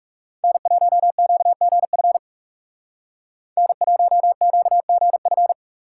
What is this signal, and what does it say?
Morse code, heard as a keyed tone, about 35 wpm, 700 Hz. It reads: N1QGF N1QGF
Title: Speed Racing - USA 1x3 Call Signs 30wpm